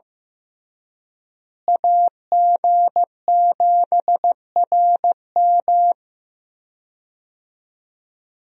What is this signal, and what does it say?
Morse code, heard as a keyed tone, about 15 wpm, 700 Hz. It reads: AG7RM
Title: USA Call Signs - 5 Characters - Repeated 3x - 15wpm